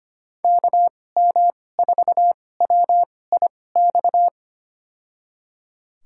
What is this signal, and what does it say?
Morse code, heard as a keyed tone, about 25 wpm, spaced as for 18 wpm, 700 Hz. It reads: KM4WIX
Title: USA Call Signs - 2x Character Spacing - 25wpm